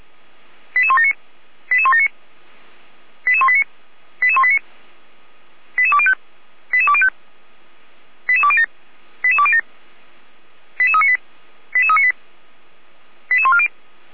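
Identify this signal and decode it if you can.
Five-tone selective call (ZVEI), heard as quick runs of five tones, tones of about 70 ms. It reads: 88179 88179 88189 88189 88205 88205 88207 88207 88398 88398 88140
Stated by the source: Five-tone ZVEI selective call received on an SDR, from the feuerwehr-orlamuende.org fire brigade audio archive.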